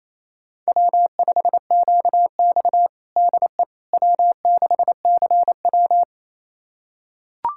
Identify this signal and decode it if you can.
Morse code, heard as a keyed tone, about 27 wpm, 700 Hz. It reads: W5QX DE W6CW
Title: USA Callsign DE Callsign - 4 Characters - 28wpm